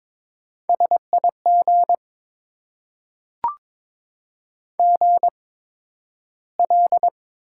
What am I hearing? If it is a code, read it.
Morse code, heard as a keyed tone, about 22 wpm, spaced as for 12 wpm, 700 Hz. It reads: SIG GL
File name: Top 90 QSO Elements - 8x Character Spacing - 22wpm